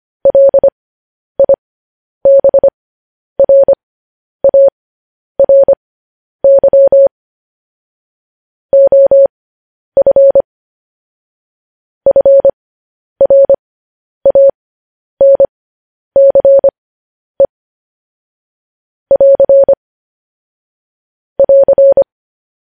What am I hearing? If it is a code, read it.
Morse code, heard as a keyed tone, about 25 wpm, spaced as for 10 wpm, 550 Hz. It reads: LIBRARY OF FRANCE <AR> <AR>
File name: podcast / News headlines-1x-25x10